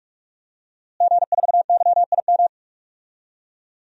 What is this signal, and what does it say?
Morse code, heard as a keyed tone, about 47 wpm, 700 Hz. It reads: G4YIM
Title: DXCC-100 Call Signs - 5 Characters - Repeated 3x - 45wpm